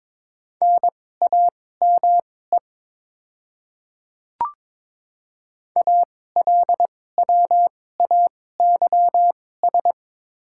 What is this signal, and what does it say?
Morse code, heard as a keyed tone, about 22 wpm, spaced as for 16 wpm, 700 Hz. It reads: NAME ALWAYS
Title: 2x Character Spacing - Top 500 Words - 22wpm